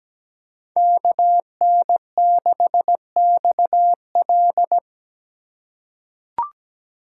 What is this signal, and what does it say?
Morse code, heard as a keyed tone, about 17 wpm, 700 Hz. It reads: KN6XL
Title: USA Call Signs - 5 Characters - 8x Character Spacing - 17wpm